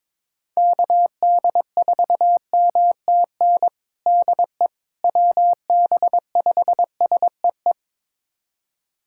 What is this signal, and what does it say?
Morse code, heard as a keyed tone, about 22 wpm, 700 Hz. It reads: KD4MTN DE WB5SEE
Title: USA Callsign DE Callsign - 6 Characters - 22wpm